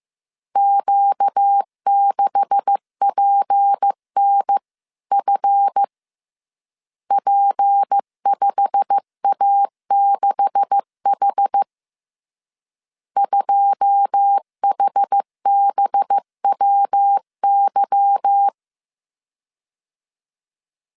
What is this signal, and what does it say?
Morse code, heard as a keyed tone, about 15 wpm, 800 Hz. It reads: Q6PNF P5A6H 2HBWY
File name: CW / DET / 1 9 1500C S C 14wpm 3245